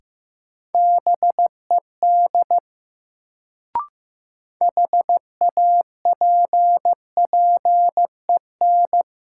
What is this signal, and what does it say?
Morse code, heard as a keyed tone, about 15 wpm, 700 Hz. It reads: BED HAPPEN
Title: Sets of 5 Words Encore - New Word Drill - 15wpm